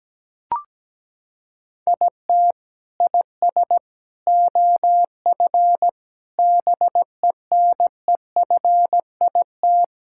Morse code, heard as a keyed tone, about 17 wpm, 700 Hz. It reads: IT IS OF BENEFIT
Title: Sentences from Top 1000 Words - Review Part 6 - 17wpm